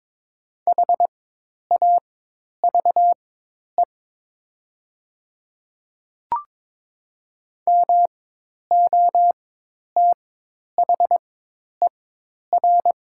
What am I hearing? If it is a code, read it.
Morse code, heard as a keyed tone, about 22 wpm, spaced as for 10 wpm, 700 Hz. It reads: HAVE MOTHER
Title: 4x Character Spacing - Top 500 Words - 22wpm